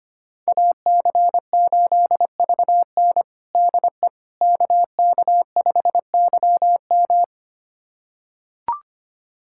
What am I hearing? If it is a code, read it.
Morse code, heard as a keyed tone, about 25 wpm, 700 Hz. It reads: AC8VN DE KK5YM
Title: USA Callsign DE Callsign - 5 Characters - 25wpm